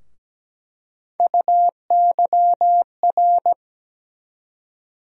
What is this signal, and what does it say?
Morse code, heard as a keyed tone, about 17 wpm, 700 Hz. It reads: UYR
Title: USA Call Sign Postfixes - 3 Letters 17wpm